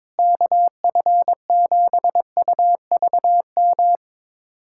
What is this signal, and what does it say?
Morse code, heard as a keyed tone, about 22 wpm, 700 Hz. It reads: KF7UVM
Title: USA Callsign DE Callsign 22wpm